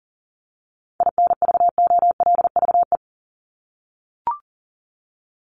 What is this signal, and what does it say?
Morse code, heard as a keyed tone, about 40 wpm, 700 Hz. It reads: INVOLVE
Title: ED Suffix - Top 100 Words with Base Words 40wpm